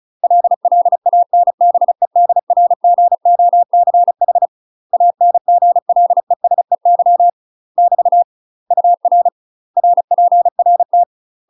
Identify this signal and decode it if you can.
Morse code, heard as a keyed tone, about 34 wpm, 700 Hz. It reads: LLANBEDRGOCH ANGLESEY <BT> UR RPRT
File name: UK MASTER QSOs TRAINING / 35WPM / UK MASTER QSOs 0000